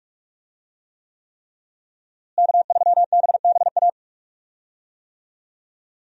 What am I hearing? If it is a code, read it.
Morse code, heard as a keyed tone, about 41 wpm, 700 Hz. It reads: K3BBA
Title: USA Call Signs - 5 Characters - Repeated 3x - 45wpm